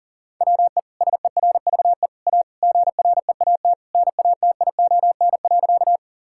Morse code, heard as a keyed tone, about 40 wpm, 700 Hz. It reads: WE SERVE A GREAT NATION.
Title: Sentences from Top 1000 Words - Review Part 6 - 40wpm